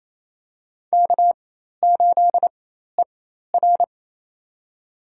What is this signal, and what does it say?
Morse code, heard as a keyed tone, about 27 wpm, spaced as for 13 wpm, 700 Hz. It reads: K8ER
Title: USA Call Signs - 4 Characters - 4x Character Spacing - 28wpm